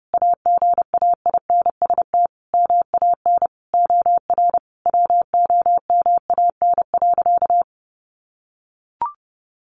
Morse code, heard as a keyed tone, about 30 wpm, 700 Hz. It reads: AGAINST MAN OR WOMAN.
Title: Sentences from Top 1400 Words 30wpm